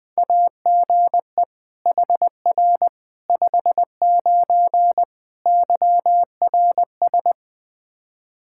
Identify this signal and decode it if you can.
Morse code, heard as a keyed tone, about 20 wpm, 700 Hz. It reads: AGE HR 59 YRS
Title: Practice QSO - Exchange 3 - More Info - 20wpm